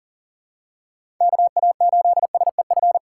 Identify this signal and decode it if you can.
Morse code, heard as a keyed tone, about 40 wpm, 700 Hz. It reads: KA8SEF
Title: USA Call Signs - 2x Character Spacing - 40wpm